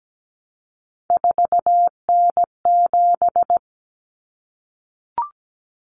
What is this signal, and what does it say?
Morse code, heard as a keyed tone, about 17 wpm, 700 Hz. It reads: 4N7
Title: Common Cut Numbers - 3-Digit 17wpm